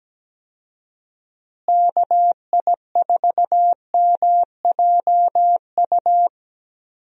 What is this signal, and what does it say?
Morse code, heard as a keyed tone, about 17 wpm, 700 Hz. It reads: KI4MJU
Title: USA Call Signs - 6 Characters - Repeated 3x - 17wpm